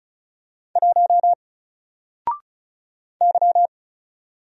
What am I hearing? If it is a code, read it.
Morse code, heard as a keyed tone, about 35 wpm, 700 Hz. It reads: 1 Y
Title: Single Letter-Number 35wpm